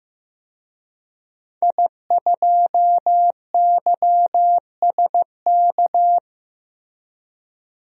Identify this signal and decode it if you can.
Morse code, heard as a keyed tone, about 15 wpm, 700 Hz. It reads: I2YSK
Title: DXCC-100 Call Signs - 5 Characters - Repeated 2x - 15wpm